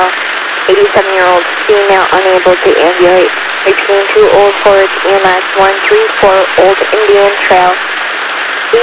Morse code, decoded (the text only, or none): none